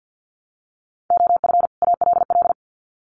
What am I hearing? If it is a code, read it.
Morse code, heard as a keyed tone, about 50 wpm, 700 Hz. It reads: OF ALL
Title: Sets of 2 Words 50wpm